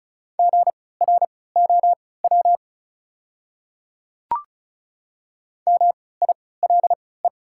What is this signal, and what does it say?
Morse code, heard as a keyed tone, about 34 wpm, spaced as for 20 wpm, 700 Hz. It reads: GROW MILE
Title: 3x Character Spacing - Top 300 Words - 35wpm